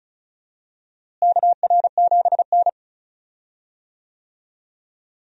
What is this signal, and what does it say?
Morse code, heard as a keyed tone, about 36 wpm, 700 Hz. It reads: KR7N
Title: USA Call Signs - 4 Characters - Repeated 2x - 35wpm